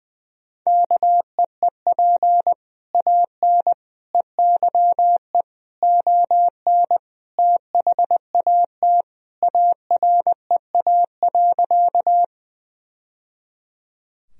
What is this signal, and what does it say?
Morse code, heard as a keyed tone, about 21 wpm, 700 Hz. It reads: KEEP AN EYE ON THAT AREA.
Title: Sentences from Top 1000 Words - Review Part 2 - 20wpm